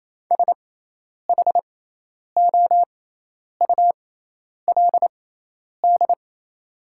Morse code, heard as a keyed tone, about 28 wpm, spaced as for 10 wpm, 700 Hz. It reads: SHOULD
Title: 6x Character Spacing - Top 300 Words - 28wpm